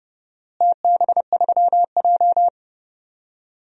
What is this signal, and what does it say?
Morse code, heard as a keyed tone, about 32 wpm, 700 Hz. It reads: TB3J
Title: DXCC-100 Call Signs - 4 Characters - Repeated 2x - 30wpm